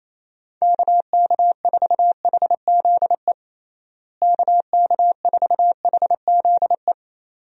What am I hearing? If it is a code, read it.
Morse code, heard as a keyed tone, about 28 wpm, 700 Hz. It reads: KK4HZE KK4HZE
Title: USA Call Signs - 6 Characters - Repeated 2x - 28wpm